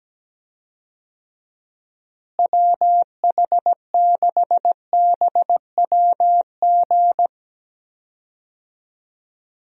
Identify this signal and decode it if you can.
Morse code, heard as a keyed tone, about 17 wpm, 700 Hz. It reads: WH6BWG
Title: USA Call Signs - 6 Characters - Repeated 3x - 17wpm